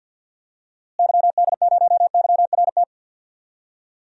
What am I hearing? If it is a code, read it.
Morse code, heard as a keyed tone, about 52 wpm, 700 Hz. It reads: YD0YRT
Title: DXCC-100 Call Signs - 6 Characters - Repeated 2x - 50wpm